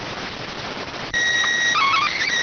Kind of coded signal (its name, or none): SSTV (slow-scan television)